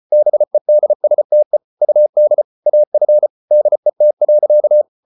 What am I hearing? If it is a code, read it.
Morse code, heard as a keyed tone, about 34 wpm, 600 Hz. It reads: BEDSTE UD AF DET.
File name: Ferie QRQ 34wpm